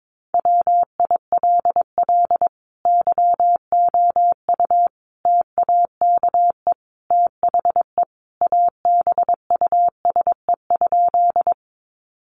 Morse code, heard as a keyed tone, about 22 wpm, 700 Hz. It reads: WILL YOU TAKE THE ABUSE?
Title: Sentences from Top 1300 Words 22wpm